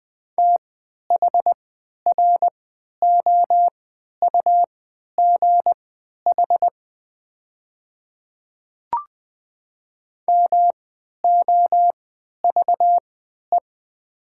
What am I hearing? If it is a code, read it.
Morse code, heard as a keyed tone, about 20 wpm, spaced as for 11 wpm, 700 Hz. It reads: THROUGH MOVE
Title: 3x Character Spacing - Top 300 Words - 20wpm